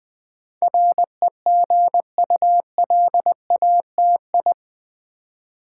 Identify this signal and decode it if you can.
Morse code, heard as a keyed tone, about 20 wpm, 700 Hz. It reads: REGULATI
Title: Build A Word - 4 Letters At A Time - 10-Letter Words - 20wpm